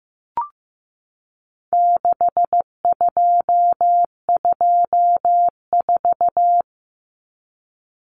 Morse code, heard as a keyed tone, about 15 wpm, 700 Hz. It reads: 6224
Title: Common Cut Numbers - 4-Digit 15wpm